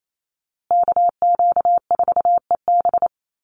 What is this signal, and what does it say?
Morse code, heard as a keyed tone, about 28 wpm, 700 Hz. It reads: KQ4EB
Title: USA Call Signs Repeated 3x 28wpm